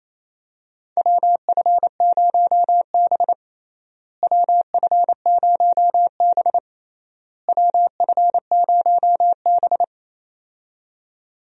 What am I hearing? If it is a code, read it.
Morse code, heard as a keyed tone, about 28 wpm, 700 Hz. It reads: WF0B WF0B WF0B
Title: USA Call Signs - 4 Characters - Repeated 3x - 28wpm